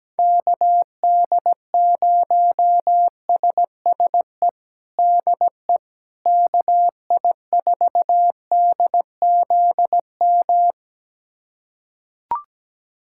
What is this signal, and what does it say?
Morse code, heard as a keyed tone, about 17 wpm, 700 Hz. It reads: KD0SSE DE KI4DZM
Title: USA Callsign DE Callsign 17wpm